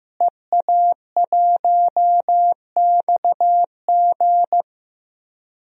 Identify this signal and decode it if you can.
Morse code, heard as a keyed tone, about 15 wpm, 700 Hz. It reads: EA1XG
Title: DXCC-100 Call Signs - 5 Characters - Repeated 2x - 15wpm